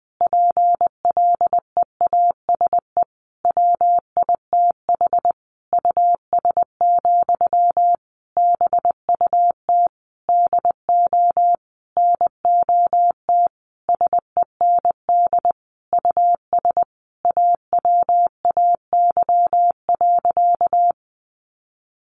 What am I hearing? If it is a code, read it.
Morse code, heard as a keyed tone, about 19 wpm, 700 Hz. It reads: PLEASE WITH US, BUT DO NOT SEND US AWAY.
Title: Sentences from Top 1500 Words 20wpm